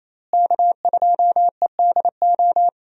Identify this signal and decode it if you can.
Morse code, heard as a keyed tone, about 27 wpm, 700 Hz. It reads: K2EDO
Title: USA Call Signs - 5 Characters - Repeated 3x - 28wpm